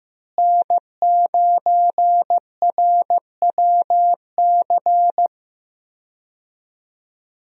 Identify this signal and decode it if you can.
Morse code, heard as a keyed tone, about 15 wpm, 700 Hz. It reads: N9RWC
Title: USA Call Signs - 5 Characters - Repeated 3x - 15wpm